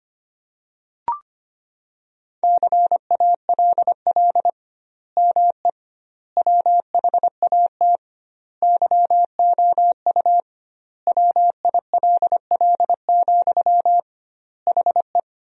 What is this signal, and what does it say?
Morse code, heard as a keyed tone, about 25 wpm, 700 Hz. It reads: CALL ME WHAT YOU WILL, HE
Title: 2x Word-Spacing - Sentences from Top 200 Words - 25wpm